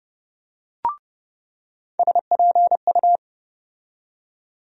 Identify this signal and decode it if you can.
Morse code, heard as a keyed tone, about 30 wpm, 700 Hz. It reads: SPU
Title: USA Call Sign Postfixes - 3 Letters 30wpm